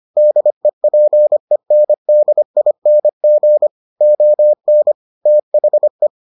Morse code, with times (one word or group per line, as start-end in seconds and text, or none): 0.2-3.7 DEPENDING
4.0-4.9 ON
5.3-6.1 THE